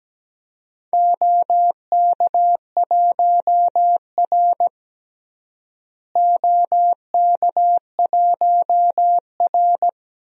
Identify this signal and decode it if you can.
Morse code, heard as a keyed tone, about 17 wpm, 700 Hz. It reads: OK1R OK1R
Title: DXCC-100 Call Signs - 4 Characters - Repeated 3x - 17wpm